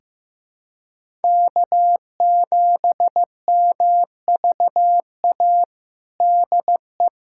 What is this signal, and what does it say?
Morse code, heard as a keyed tone, about 16 wpm, 700 Hz. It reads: K7MVA DE
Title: USA Callsign DE Callsign - 5 Characters - 15wpm